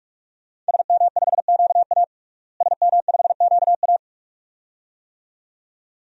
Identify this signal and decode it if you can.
Morse code, heard as a keyed tone, about 46 wpm, 700 Hz. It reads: SM5QA SM5QA
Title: DXCC-100 Call Signs - 5 Characters - Repeated 2x - 45wpm